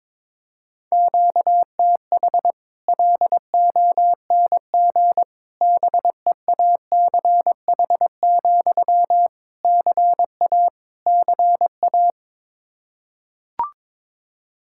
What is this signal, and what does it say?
Morse code, heard as a keyed tone, about 22 wpm, 700 Hz. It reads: QTH LONG BEACH, CA CA
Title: Practice QSO - Exchange 1 - Beginning - 22wpm